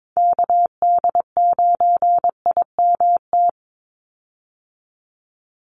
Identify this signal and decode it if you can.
Morse code, heard as a keyed tone, about 22 wpm, 700 Hz. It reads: KD9IMT
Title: USA Call Signs 22wpm